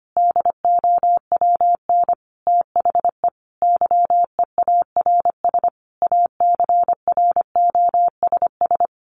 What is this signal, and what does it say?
Morse code, heard as a keyed tone, about 25 wpm, 700 Hz. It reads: DOWN THE YEARS ACROSS
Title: A Sound Of Thunder 25wpm